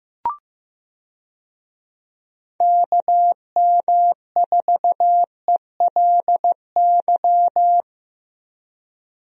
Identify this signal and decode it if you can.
Morse code, heard as a keyed tone, about 14 wpm, 700 Hz. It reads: KM4ELY
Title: USA Call Signs - 6 Characters - Repeated 2x - 15wpm